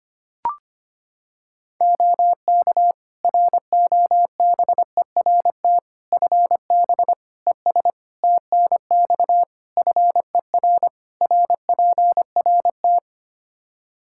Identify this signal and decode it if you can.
Morse code, heard as a keyed tone, about 25 wpm, 700 Hz. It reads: OK ROBERT FB ES TNX FER RPRT
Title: K1USN SST QSO Practice 25wpm